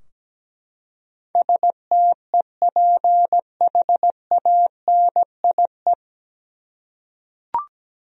Morse code, heard as a keyed tone, about 17 wpm, 700 Hz. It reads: STEPHANIE
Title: USA Common Names - 17wpm